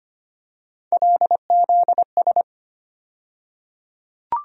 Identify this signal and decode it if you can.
Morse code, heard as a keyed tone, about 25 wpm, 700 Hz. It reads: LZS
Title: USA Call Sign Postfixes - 3 Letters 25wpm